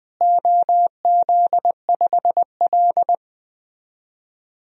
Morse code, heard as a keyed tone, about 21 wpm, 700 Hz. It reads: OZ5L
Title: DXCC-100 Call Signs - 4 Characters - Repeated 3x - 20wpm